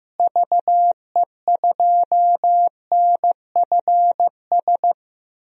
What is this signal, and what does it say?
Morse code, heard as a keyed tone, about 15 wpm, 700 Hz. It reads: VE2NFS
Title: DXCC-100 Call Signs - 6 Characters - Repeated 2x - 15wpm